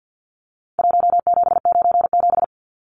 Morse code, heard as a keyed tone, about 49 wpm, 700 Hz. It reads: 1797
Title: Numbers - Contest Style - 4-Digit 50wpm